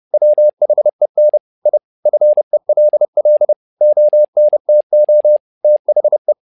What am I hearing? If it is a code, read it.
Morse code, heard as a keyed tone, about 30 wpm, 600 Hz. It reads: WHEN I FELL ONTO THE
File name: learning center / tower trouble / tower trouble 30wpm